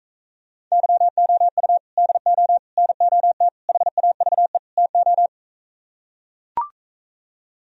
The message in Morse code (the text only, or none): YOU DO NOT HAVE TO